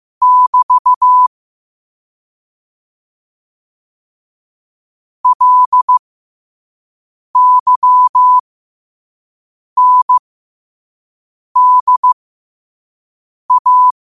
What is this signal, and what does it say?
Morse code, heard as a keyed tone, about 15 wpm, spaced as for 5 wpm, 1000 Hz. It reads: <BT> LYNDA
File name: qotdfiles / 5wpm